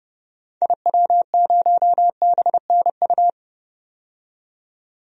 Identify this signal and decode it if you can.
Morse code, heard as a keyed tone, about 30 wpm, 700 Hz. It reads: IW0BNU
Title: DXCC-100 Call Signs - 6 Characters - 30wpm